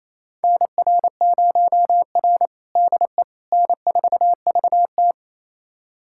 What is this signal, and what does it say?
Morse code, heard as a keyed tone, about 28 wpm, 700 Hz. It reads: NR0R DE N4VT
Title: USA Callsign DE Callsign - 4 Characters - 28wpm